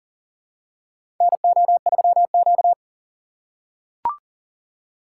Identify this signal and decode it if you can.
Morse code, heard as a keyed tone, about 40 wpm, 700 Hz. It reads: NO3Q
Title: USA Call Signs - 6x Character Spacing - 40wpm